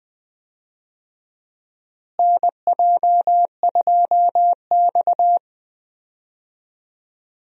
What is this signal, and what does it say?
Morse code, heard as a keyed tone, about 20 wpm, 700 Hz. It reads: NJ2X
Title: USA Call Signs - 4 Characters - Repeated 3x - 20wpm